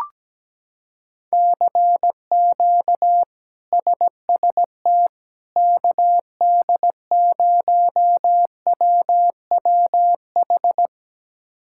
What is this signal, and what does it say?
Morse code, heard as a keyed tone, about 17 wpm, 700 Hz. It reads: CQ SST KD0WWH
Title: K1USN SST QSO Practice - 17wpm